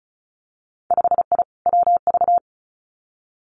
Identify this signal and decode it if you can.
Morse code, heard as a keyed tone, about 36 wpm, 700 Hz. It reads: 5I WV
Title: Winter Field Day Exchanges Encore 35wpm